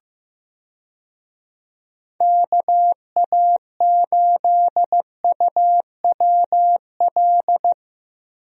Morse code, heard as a keyed tone, about 15 wpm, 700 Hz. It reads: KA8UWL
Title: USA Call Signs - 6 Characters - Repeated 3x - 15wpm